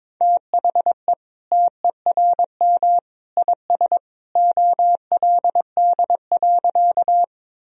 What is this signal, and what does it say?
Morse code, heard as a keyed tone, about 22 wpm, 700 Hz. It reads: THE TERM IS OLD.
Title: Sentences from Top 1000 Words - Review Part 4 - 22wpm